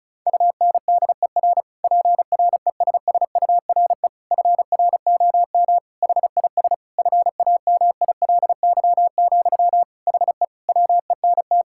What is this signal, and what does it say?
Morse code, heard as a keyed tone, about 36 wpm, 700 Hz. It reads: UNDER PRESSURE FROM HIS FAMILY, HE WENT
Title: Sentences from Top 1000 Words - Review Part 6 - 35wpm